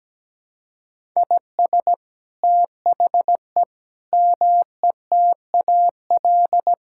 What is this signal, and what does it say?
Morse code, heard as a keyed tone, about 17 wpm, 700 Hz. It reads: IS THE METAL
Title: Sentences from Top 1000 Words - Review Part 6 - 17wpm